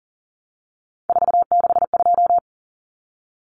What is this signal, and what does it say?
Morse code, heard as a keyed tone, about 40 wpm, 700 Hz. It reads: 462
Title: Common Cut Numbers - 3-Digit 40wpm